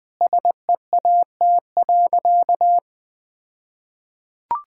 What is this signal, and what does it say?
Morse code, heard as a keyed tone, about 21 wpm, 700 Hz. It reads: SEAT.
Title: All-Age Appropriate Jokes - 20wpm